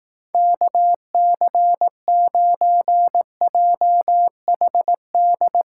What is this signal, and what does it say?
Morse code, heard as a keyed tone, about 18 wpm, 700 Hz. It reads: KC9JHD
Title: Speed Racing - USA 2x3 Call Signs 15wpm